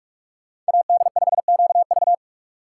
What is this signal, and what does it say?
Morse code, heard as a keyed tone, about 65 wpm, spaced as for 55 wpm, 700 Hz. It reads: AD5QV
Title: Speed Racing - USA 2x2 Call Signs 45wpm